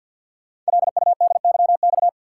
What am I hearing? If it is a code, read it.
Morse code, heard as a keyed tone, about 46 wpm, 700 Hz. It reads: LUDYX
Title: Code Groups - 5 Letters - 50wpm